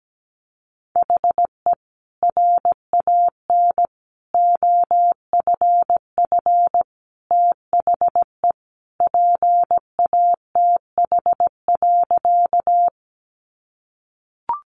Morse code, heard as a keyed tone, about 17 wpm, 700 Hz. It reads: HE RAN OFF THE PATH.
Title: A Sound Of Thunder 17wpm